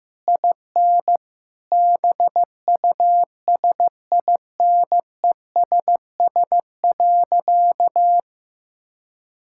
Morse code, heard as a keyed tone, about 15 wpm, 700 Hz. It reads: IN BUSINESS.
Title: Sentences from Top 1000 Words - Review Part 6 - 15wpm